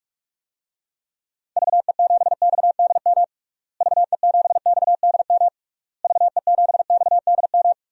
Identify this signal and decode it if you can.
Morse code, heard as a keyed tone, about 41 wpm, 700 Hz. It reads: VE7XDM VE7XDM VE7XDM
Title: DXCC-100 Call Signs - 6 Characters - Repeated 3x - 45wpm